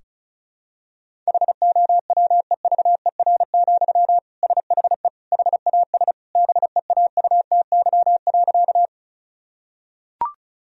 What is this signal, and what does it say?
Morse code, heard as a keyed tone, about 36 wpm, 700 Hz. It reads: HOWEVER, SHE HAS BEAUTY.
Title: Sentences from Top 1000 Words - Review Part 2 - 35wpm